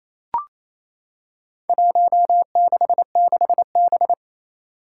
Morse code, heard as a keyed tone, about 28 wpm, 700 Hz. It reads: 166B
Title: Four Challenging Letter-Number Combinations 28wpm